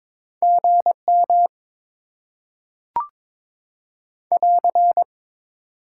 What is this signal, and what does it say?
Morse code, heard as a keyed tone, about 22 wpm, 700 Hz. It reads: GM <AR>
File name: Top 90 QSO Elements - 6x Character Spacing - 22wpm